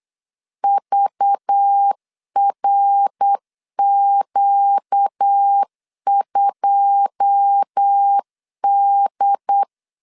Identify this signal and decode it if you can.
Morse code, heard as a keyed tone, about 8 wpm, 800 Hz. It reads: VRQ2D
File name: CW / DET / 1 9 1500C S B 8wpm 5624